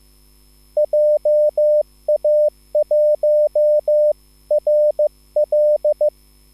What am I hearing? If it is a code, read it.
Morse code, heard as a keyed tone, about 15 wpm, 600 Hz. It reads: JA1RL
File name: QSL / CW lesson